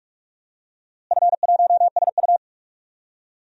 Morse code, heard as a keyed tone, about 43 wpm, 700 Hz. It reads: F1SU
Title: DXCC-100 Call Signs - 4 Characters - Repeated 2x - 45wpm